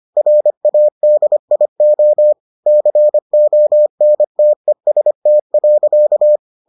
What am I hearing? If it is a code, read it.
Morse code, heard as a keyed tone, about 25 wpm, 600 Hz. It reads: RADIO CONTEST.